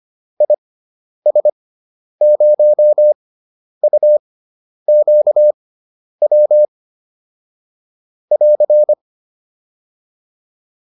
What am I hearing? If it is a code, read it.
Morse code, heard as a keyed tone, about 25 wpm, spaced as for 10 wpm, 600 Hz. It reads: IS0UQW <AR>
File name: QSO409 10